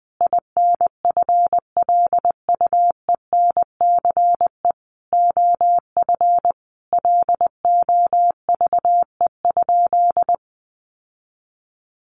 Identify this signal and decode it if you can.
Morse code, heard as a keyed tone, about 20 wpm, 700 Hz. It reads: INFLUENCE OF LOVE?
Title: Sentences from Top 1100 Words 20wpm